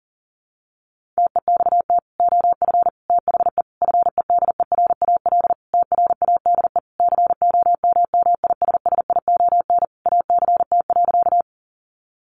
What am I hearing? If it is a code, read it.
Morse code, heard as a keyed tone, about 40 wpm, 700 Hz. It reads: TEXT OF THE FEDERAL TRADE COMMISSION ACT.
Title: Sentences from Top 1400 Words 40wpm